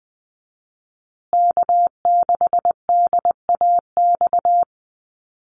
Morse code, heard as a keyed tone, about 20 wpm, 700 Hz. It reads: K6DAX
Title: USA Call Signs Repeated 3x 20wpm